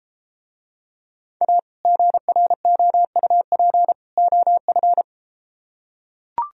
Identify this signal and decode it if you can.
Morse code, heard as a keyed tone, about 33 wpm, 700 Hz. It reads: A GROUP OF